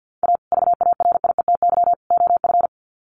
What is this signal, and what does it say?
Morse code, heard as a keyed tone, about 50 wpm, 700 Hz. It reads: A VARIETY OF
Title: Sets of 3 Words 50wpm